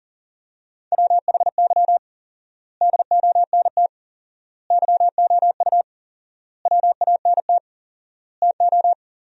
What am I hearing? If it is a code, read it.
Morse code, heard as a keyed tone, about 40 wpm, 700 Hz. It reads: WHY DONT YOU WANT TO
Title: 4x Word-Spacing - Sentences from Top 300 Words - 40wpm